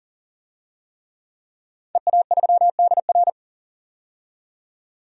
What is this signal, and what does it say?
Morse code, heard as a keyed tone, about 40 wpm, 700 Hz. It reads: EA3DR
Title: DXCC-100 Call Signs - 5 Characters - 40wpm